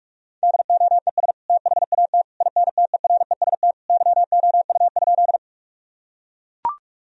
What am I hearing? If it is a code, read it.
Morse code, heard as a keyed tone, about 41 wpm, 700 Hz. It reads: DOES THAT INTEREST YOU?
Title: Sentences from Top 1000 Words - Review Part 2 - 45wpm